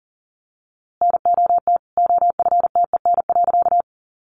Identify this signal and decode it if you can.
Morse code, heard as a keyed tone, about 40 wpm, 700 Hz. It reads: NOT OFTEN.
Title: A Sound Of Thunder 40wpm